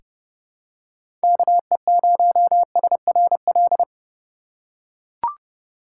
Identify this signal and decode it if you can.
Morse code, heard as a keyed tone, about 31 wpm, 700 Hz. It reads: KE0SRL
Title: K1USN SST QSO Practice - 30wpm